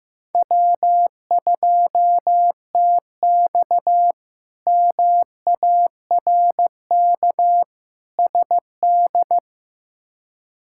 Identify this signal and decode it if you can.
Morse code, heard as a keyed tone, about 15 wpm, 700 Hz. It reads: W2TX MARK SD
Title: K1USN SST QSO Practice - 15wpm